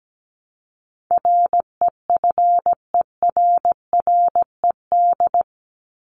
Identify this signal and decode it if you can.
Morse code, heard as a keyed tone, about 17 wpm, 700 Hz. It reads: REFERRED
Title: ED Suffix - Top 100 Words with Base Words 17wpm